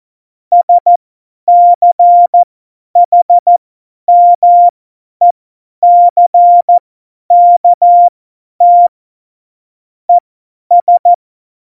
Morse code, heard as a keyed tone, about 14 wpm, spaced as for 10 wpm, 700 Hz. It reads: SCHMECKT ES
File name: Morsekurs / Witze / Witz 18. Juni 2019, 50 F70